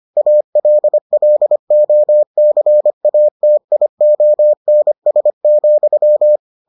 Morse code, heard as a keyed tone, about 25 wpm, 600 Hz. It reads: ALLOCATIONS,